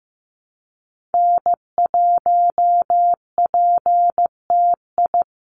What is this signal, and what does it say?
Morse code, heard as a keyed tone, about 15 wpm, 700 Hz. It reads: N1PTI
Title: USA Call Signs 15wpm